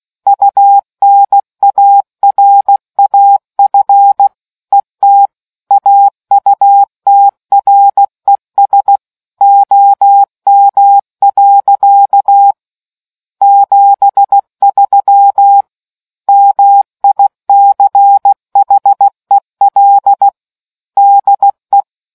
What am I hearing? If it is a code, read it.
Morse code, heard as a keyed tone, about 16 wpm, 800 Hz. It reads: UNARAF ET AUTRES OM. 73 MICHEL DE